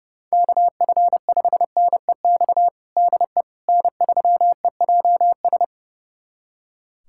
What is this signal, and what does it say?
Morse code, heard as a keyed tone, about 30 wpm, 700 Hz. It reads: KF5NEX DE N3EJS
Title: USA Callsign DE Callsign 30wpm